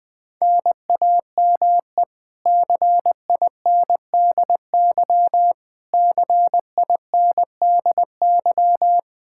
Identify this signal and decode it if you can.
Morse code, heard as a keyed tone, about 21 wpm, 700 Hz. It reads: NAME CINDY CINDY
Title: K1USN SST QSO Practice 20wpm